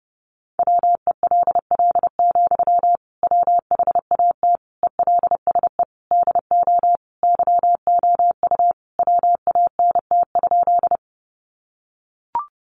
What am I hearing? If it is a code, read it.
Morse code, heard as a keyed tone, about 30 wpm, 700 Hz. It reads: WELL, WHAT ELSE DO YOU WANT?
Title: Sentences from Top 800 Words 30wpm